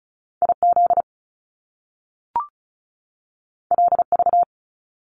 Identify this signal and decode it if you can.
Morse code, heard as a keyed tone, about 36 wpm, 700 Hz. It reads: IZ LV
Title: Top Two Letter Combinations in Words 35wpm